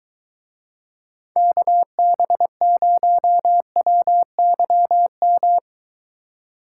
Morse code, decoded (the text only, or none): KB0WYM